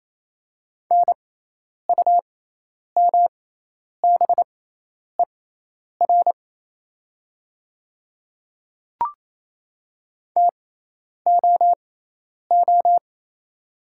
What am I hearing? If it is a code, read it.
Morse code, heard as a keyed tone, about 27 wpm, spaced as for 10 wpm, 700 Hz. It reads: NUMBER TOO
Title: 6x Character Spacing - Top 200 Words - 28wpm